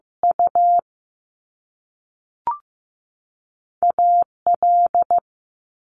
Morse code, heard as a keyed tone, about 15 wpm, 700 Hz. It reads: U A L
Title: Top Two Letter Combinations in Words 15wpm